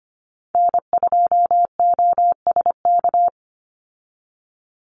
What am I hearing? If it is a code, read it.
Morse code, heard as a keyed tone, about 25 wpm, 700 Hz. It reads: N2OSK
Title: USA Call Signs 25wpm